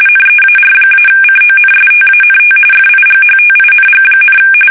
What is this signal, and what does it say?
RTTY, about 45.45 baud. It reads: NEAR-CONSTANT FIGHTING,
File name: fsk850 / battleshipnj